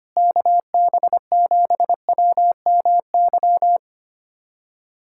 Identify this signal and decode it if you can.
Morse code, heard as a keyed tone, about 25 wpm, 700 Hz. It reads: KB7WMY
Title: USA Callsign DE Callsign - 6 Characters - 25wpm